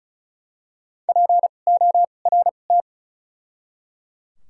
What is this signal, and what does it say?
Morse code, heard as a keyed tone, about 34 wpm, spaced as for 25 wpm, 700 Hz. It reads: PORT
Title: 2x Character Spacing - Top 500 Words - 35wpm